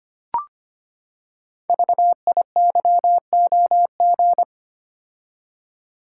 Morse code, heard as a keyed tone, about 25 wpm, 700 Hz. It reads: VIYOG
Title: Code Groups - 5 Letters - 25wpm